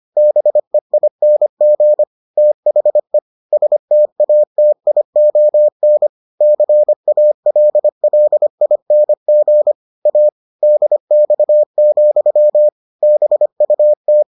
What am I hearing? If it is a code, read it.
Morse code, heard as a keyed tone, about 25 wpm, 600 Hz. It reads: BEING THE STATION CALLING A DX, BUT